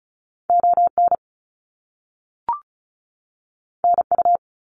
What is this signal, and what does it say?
Morse code, heard as a keyed tone, about 36 wpm, 700 Hz. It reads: ON NU
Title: Canadian Provinces and Territories 35wpm